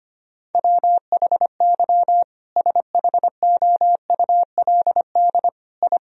Morse code, heard as a keyed tone, about 25 wpm, 700 Hz. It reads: WHY SHOULD I
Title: Sentences from Top 1000 Words - Review Part 1 - 25wpm